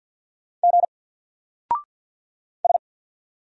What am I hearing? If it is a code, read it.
Morse code, heard as a keyed tone, about 50 wpm, 700 Hz. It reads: G S
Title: Single Character (Letter:Number:Punctuation:Prosign) 50wpm